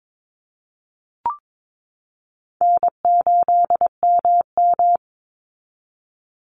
Morse code, heard as a keyed tone, about 22 wpm, 700 Hz. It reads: N8MM
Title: USA Call Signs 22wpm